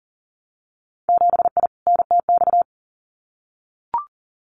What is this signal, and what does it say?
Morse code, heard as a keyed tone, about 42 wpm, 700 Hz. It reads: 7I NTX
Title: Winter Field Day Exchanges Encore 40wpm